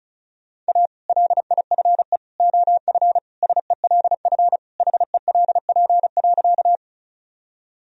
Morse code, heard as a keyed tone, about 34 wpm, 700 Hz. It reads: A LIFE OF SELF HELP.
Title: Sentences from Top 1000 Words - Review Part 1 - 35wpm